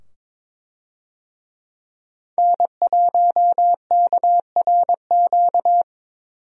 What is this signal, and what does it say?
Morse code, heard as a keyed tone, about 23 wpm, 700 Hz. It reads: N1KRQ
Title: USA Call Signs - 5 Characters - 22wpm